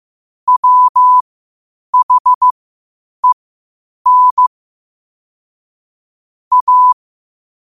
Morse code, heard as a keyed tone, about 15 wpm, spaced as for 8 wpm, 1000 Hz. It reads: WHEN A
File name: qotdfiles / 8wpm